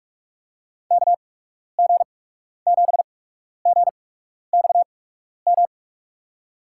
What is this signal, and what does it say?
Morse code, heard as a keyed tone, about 48 wpm, spaced as for 12 wpm, 700 Hz. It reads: KG7GXM
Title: USA Call Signs - 6 Characters - 8x Character Spacing - 45wpm